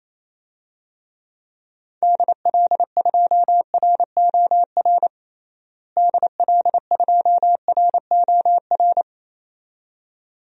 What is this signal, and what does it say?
Morse code, heard as a keyed tone, about 28 wpm, 700 Hz. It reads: DL2ROR DL2ROR
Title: DXCC-100 Call Signs - 6 Characters - Repeated 2x - 28wpm